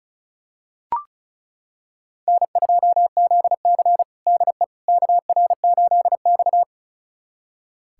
Morse code, heard as a keyed tone, about 36 wpm, 700 Hz. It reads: N2ZC DE KR8X
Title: USA Callsign DE Callsign - 4 Characters - 35wpm